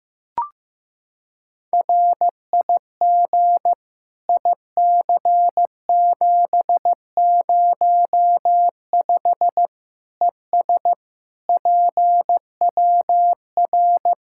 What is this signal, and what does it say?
Morse code, heard as a keyed tone, about 15 wpm, 700 Hz. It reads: RIG IC705 ES PWR
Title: Practice QSO - Exchange 2 - Info - 15wpm